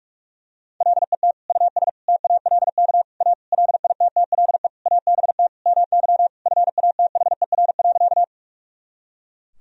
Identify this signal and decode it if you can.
Morse code, heard as a keyed tone, about 42 wpm, 700 Hz. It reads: LET US TALK A LITTLE ABT MY FATHER.
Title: Sentences from Top 400 Words and Abbreviations 45wpm